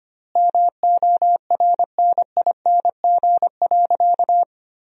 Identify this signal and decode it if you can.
Morse code, heard as a keyed tone, about 25 wpm, 700 Hz. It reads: MORNING.
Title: 3x Word-Spacing - Sentences from Top 400 Words - 25wpm